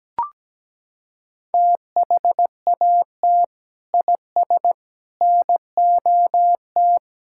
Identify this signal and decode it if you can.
Morse code, heard as a keyed tone, about 17 wpm, 700 Hz. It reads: THAT IS NOT
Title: Sentences from Top 1000 Words - Review Part 6 - 17wpm